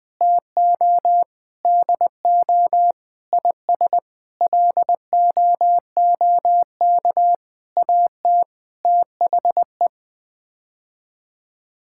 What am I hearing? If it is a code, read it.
Morse code, heard as a keyed tone, about 20 wpm, 700 Hz. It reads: TO DO IS LOOK AT THE
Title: Sets of 9 Words - 20wpm